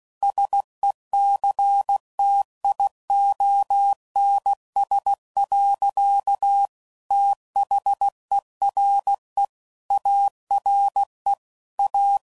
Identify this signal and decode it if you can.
Morse code, heard as a keyed tone, about 16 wpm, 800 Hz. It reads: SECTIONS. THERE ARE A
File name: CW Code Groups @15WPM/Technical Subjects